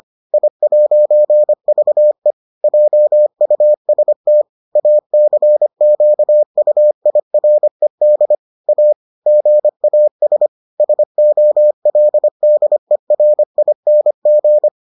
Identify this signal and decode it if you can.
Morse code, heard as a keyed tone, about 25 wpm, 600 Hz. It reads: I'VE JUST ACQUIRED A GAS SOLDERING